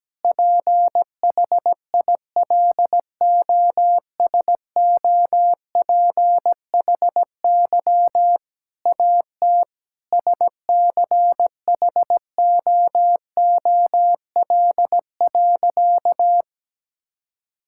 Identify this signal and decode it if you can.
Morse code, heard as a keyed tone, about 17 wpm, 700 Hz. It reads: PHILOSOPHY AT SCHOOL.
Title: Sentences from Top 1000 Words - Review Part 6 - 17wpm